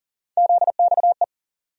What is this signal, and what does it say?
Morse code, heard as a keyed tone, about 40 wpm, 700 Hz. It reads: ZXE
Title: USA Call Sign Postfixes - 3 Letters 40wpm